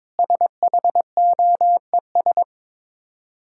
Morse code, heard as a keyed tone, about 22 wpm, 700 Hz. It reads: SHOES
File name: Easy Words - Speed-Racing with Hints - 22wpm